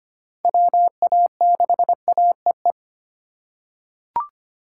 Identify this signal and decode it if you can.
Morse code, heard as a keyed tone, about 25 wpm, 700 Hz. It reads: WA6AEE
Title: K1USN SST QSO Practice - 25wpm